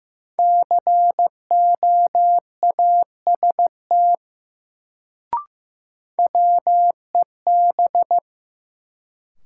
Morse code, heard as a keyed tone, about 15 wpm, 700 Hz. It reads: COAST WEB
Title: Sentences from Top 1000 Words - Review Part 4 - New Word Drill - 15wpm